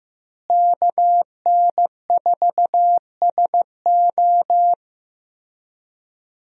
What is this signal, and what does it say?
Morse code, heard as a keyed tone, about 15 wpm, 700 Hz. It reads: KN4SO
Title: USA Call Signs - 5 Characters - Repeated 3x - 15wpm